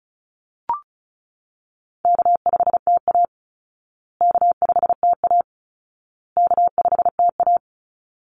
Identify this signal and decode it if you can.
Morse code, heard as a keyed tone, about 34 wpm, 700 Hz. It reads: K5TA K5TA K5TA
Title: Call Signs Repeated 2x - 35wpm